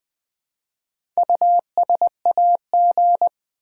Morse code, heard as a keyed tone, about 20 wpm, 700 Hz. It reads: USAG
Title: Build A Word - 1 Letter At A Time - 5-Letter Words - 20wpm